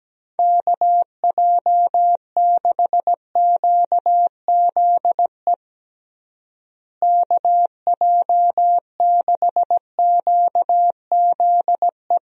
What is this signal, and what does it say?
Morse code, heard as a keyed tone, about 17 wpm, 700 Hz. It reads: KJ6QZE KJ6QZE
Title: USA Call Signs - 6 Characters - Repeated 2x - 17wpm